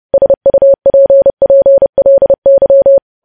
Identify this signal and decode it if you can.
Morse code, heard as a keyed tone, about 30 wpm, 550 Hz. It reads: SUPPLY
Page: news headlines in morse code at 30 wpm